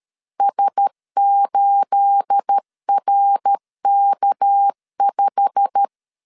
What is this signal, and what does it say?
Morse code, heard as a keyed tone, about 13 wpm, 800 Hz. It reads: S8RK5
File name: CW / DET / 1 9 1500C S A 12wpm 3800